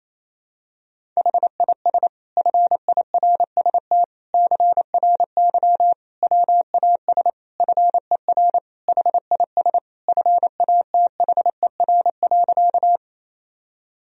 Morse code, heard as a keyed tone, about 28 wpm, 700 Hz. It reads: HIS FIRST CRY WAS FER HIS FATHER.
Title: Sentences from Top 500 Words and Abbreviations 28wpm